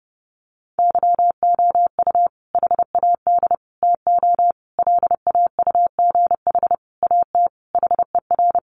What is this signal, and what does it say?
Morse code, heard as a keyed tone, about 30 wpm, 700 Hz. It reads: YOU HAD TO LAUGH AT HER
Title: Sentences from Top 600 Words 30wpm